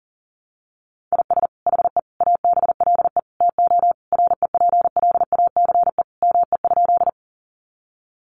Morse code, heard as a keyed tone, about 37 wpm, 700 Hz. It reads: IS HE ABLE TO REPLACE ME?
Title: Sentences from Top 1400 Words 40wpm